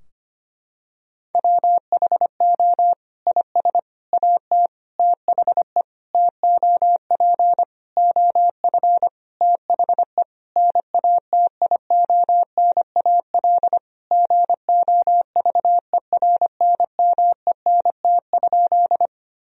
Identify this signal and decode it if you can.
Morse code, heard as a keyed tone, about 25 wpm, 700 Hz. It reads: WHO IS AT THE TOP OF THE NATIONAL GOVERNMENT?
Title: Sentences from Top 1000 Words - Review Part 4 - 25wpm